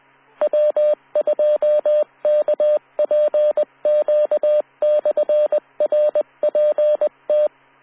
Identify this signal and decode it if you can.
Morse code, heard as a keyed tone, about 20 wpm, 600 Hz. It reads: W2KPQ/RPT